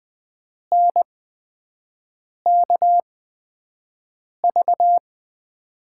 Morse code, heard as a keyed tone, about 20 wpm, spaced as for 5 wpm, 700 Hz. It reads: NKV
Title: USA Call Signs - 5 Characters - 8x Character Spacing - 20wpm